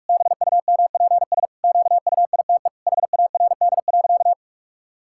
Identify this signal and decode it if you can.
Morse code, heard as a keyed tone, about 46 wpm, 700 Hz. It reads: BUMPS QUITE HARD.
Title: Pangram Challenge v2 - 45wpm